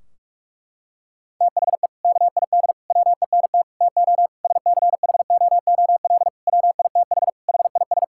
Morse code, heard as a keyed tone, about 46 wpm, 700 Hz. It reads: THE KID WENT TO SCHOOL WITH HIS
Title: Sentences from Top 1600 Words 45wpm